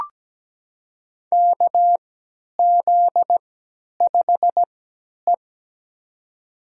Morse code, heard as a keyed tone, about 17 wpm, spaced as for 10 wpm, 700 Hz. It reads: KZ5E
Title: USA Call Signs - 4 Characters - 3x Character Spacing - 17wpm